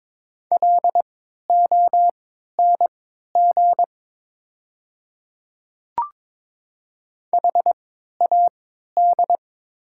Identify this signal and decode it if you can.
Morse code, heard as a keyed tone, about 22 wpm, spaced as for 12 wpm, 690 Hz. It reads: LONG HAD
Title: 3x Character Spacing - Top 100 Words - 22wpm